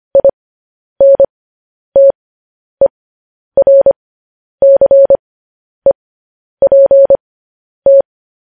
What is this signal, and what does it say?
Morse code, heard as a keyed tone, about 25 wpm, spaced as for 10 wpm, 550 Hz. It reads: INTERCEPT
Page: News Headlines in Morse Code at 10 WPM